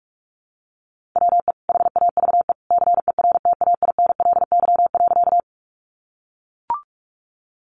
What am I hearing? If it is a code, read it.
Morse code, heard as a keyed tone, about 44 wpm, 700 Hz. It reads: WE HAVE CERTAINLY.
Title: Sentences from Top 900 Words 45wpm